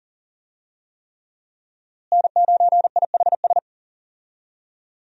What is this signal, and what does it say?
Morse code, heard as a keyed tone, about 40 wpm, 700 Hz. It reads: N9IHS
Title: USA Call Signs - 5 Characters - Repeated 3x - 40wpm